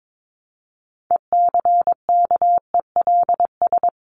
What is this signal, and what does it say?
Morse code, heard as a keyed tone, about 22 wpm, 700 Hz. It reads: ECKELS
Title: A Sound Of Thunder 22wpm